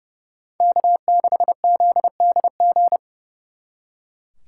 Morse code, heard as a keyed tone, about 37 wpm, 700 Hz. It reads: K6ZDG
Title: Speed Racing - USA 1x3 Call Signs 30wpm